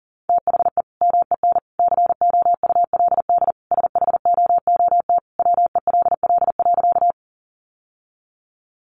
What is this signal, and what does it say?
Morse code, heard as a keyed tone, about 40 wpm, 700 Hz. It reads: THE MEN COULD SHOOT WELL.
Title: Sentences from Top 1200 Words 40wpm